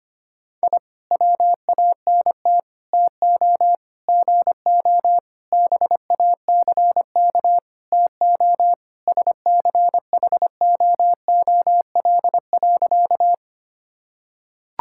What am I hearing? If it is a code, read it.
Morse code, heard as a keyed tone, about 25 wpm, 700 Hz. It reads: I WANT TO GO BACK TO SCHOOL.
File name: practice / 25wpm / Sentences with 4-Word Phrases - 25wpm